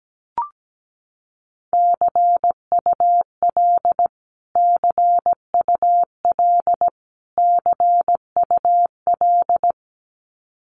Morse code, heard as a keyed tone, about 17 wpm, 700 Hz. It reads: CUL CUL CUL
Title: Abbreviations Repeated 3x - 17wpm